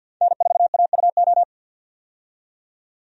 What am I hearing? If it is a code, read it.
Morse code, heard as a keyed tone, about 49 wpm, 700 Hz. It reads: N4AUO
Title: USA Call Signs - 5 Characters - 50wpm